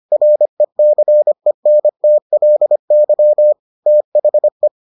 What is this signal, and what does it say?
Morse code, heard as a keyed tone, about 25 wpm, 600 Hz. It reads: RECENTLY THE